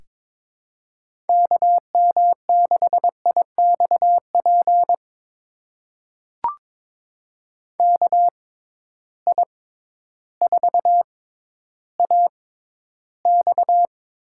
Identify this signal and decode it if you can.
Morse code, heard as a keyed tone, about 22 wpm, spaced as for 12 wpm, 700 Hz. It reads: KM6IXP KI4AX
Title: USA Call Signs - 6 Characters - 6x Character Spacing - 22wpm